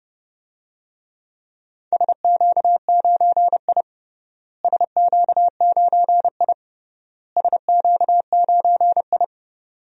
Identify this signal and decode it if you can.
Morse code, heard as a keyed tone, about 32 wpm, 700 Hz. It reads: SQ9I SQ9I SQ9I
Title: DXCC-100 Call Signs - 4 Characters - Repeated 3x - 30wpm